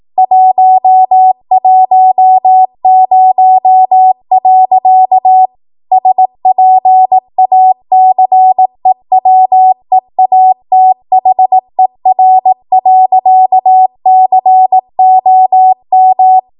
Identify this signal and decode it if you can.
Morse code, heard as a keyed tone, about 18 wpm, 750 Hz. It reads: 110. SPACEWEATHER.COM
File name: Archive / 18 WPM / 18WPM